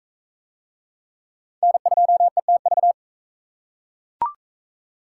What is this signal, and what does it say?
Morse code, heard as a keyed tone, about 39 wpm, 700 Hz. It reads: N2ETV